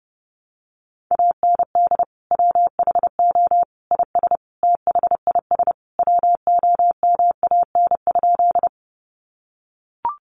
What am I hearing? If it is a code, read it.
Morse code, heard as a keyed tone, about 30 wpm, 700 Hz. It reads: AND WHO IS THIS WOMAN?
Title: Sentences from Top 1100 Words 30wpm